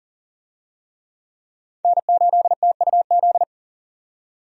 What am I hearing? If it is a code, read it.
Morse code, heard as a keyed tone, about 40 wpm, 700 Hz. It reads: N8TUZ
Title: USA Call Signs - 5 Characters - 40wpm